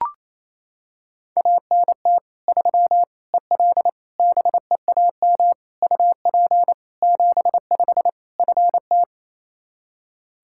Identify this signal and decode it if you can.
Morse code, heard as a keyed tone, about 28 wpm, 700 Hz. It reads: ANT 3 EL BEAM UP 75 FT
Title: Practice QSO - Exchange 2 - Info - 28wpm